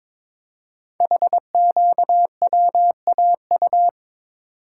Morse code, heard as a keyed tone, about 23 wpm, 700 Hz. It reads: HQWAU
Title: Code Groups - 5 Letters - 22wpm